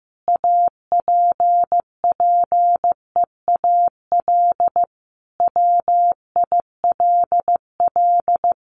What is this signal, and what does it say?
Morse code, heard as a keyed tone, about 15 wpm, 700 Hz. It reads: APPEAL WILL
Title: Sentences from Top 1300 Words 15wpm